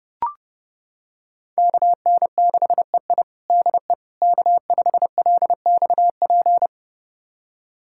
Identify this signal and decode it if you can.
Morse code, heard as a keyed tone, about 30 wpm, 700 Hz. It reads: KN6EI DE K5LXP
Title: USA Callsign DE Callsign - 5 Characters - 30wpm